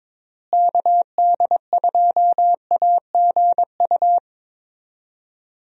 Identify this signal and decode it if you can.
Morse code, heard as a keyed tone, about 22 wpm, 700 Hz. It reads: KD2AGU
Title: USA Call Signs - 6 Characters - Repeated 2x - 22wpm